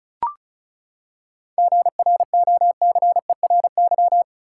Morse code, heard as a keyed tone, about 36 wpm, 700 Hz. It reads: GROCERY
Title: Seven Letter Words - 35wpm